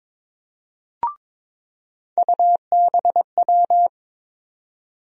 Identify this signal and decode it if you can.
Morse code, heard as a keyed tone, about 22 wpm, 700 Hz. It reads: UBW
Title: USA Call Sign Postfixes - 3 Letters 22wpm